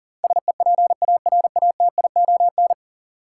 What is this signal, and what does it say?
Morse code, heard as a keyed tone, about 40 wpm, 700 Hz. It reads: SEPARATION
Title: Sets of 5 Words - New Word Drill - 40wpm